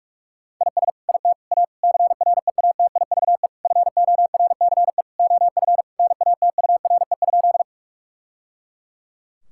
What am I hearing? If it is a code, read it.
Morse code, heard as a keyed tone, about 41 wpm, 700 Hz. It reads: IS IT A CREATIVE FORCE OF NATURE?
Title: Sentences from Top 1000 Words - Review Part 5 - 45wpm